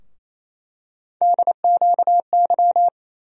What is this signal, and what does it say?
Morse code, heard as a keyed tone, about 28 wpm, 700 Hz. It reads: DQY
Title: USA Call Sign Postfixes - 3 Letters 28wpm